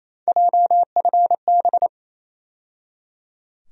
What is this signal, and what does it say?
Morse code, heard as a keyed tone, about 28 wpm, 700 Hz. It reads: JFB
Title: USA Call Sign Postfixes - 3 Letters 28wpm